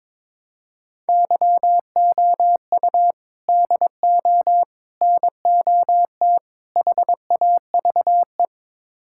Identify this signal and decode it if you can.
Morse code, heard as a keyed tone, about 22 wpm, 700 Hz. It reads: YOU DO NOT HAVE
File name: practice / 22wpm / Build A Phrase - 5-Word Phrases - 22wpm